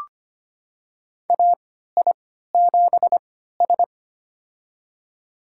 Morse code, heard as a keyed tone, about 25 wpm, spaced as for 14 wpm, 700 Hz. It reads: AI7S
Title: USA Call Signs - 4 Characters - 3x Character Spacing - 25wpm